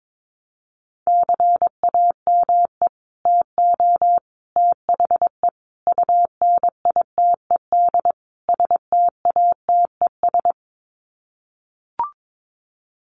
Morse code, heard as a keyed tone, about 22 wpm, 700 Hz. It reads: CAME TO THE UNITED STATES
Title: Sets of 5 Words Encore2 22wpm